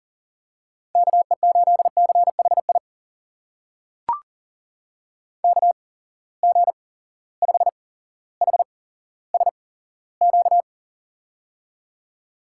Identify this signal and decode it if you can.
Morse code, heard as a keyed tone, about 42 wpm, spaced as for 17 wpm, 700 Hz. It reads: KE8CHI KG5HSQ
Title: USA Call Signs - 6 Characters - 8x Character Spacing - 40wpm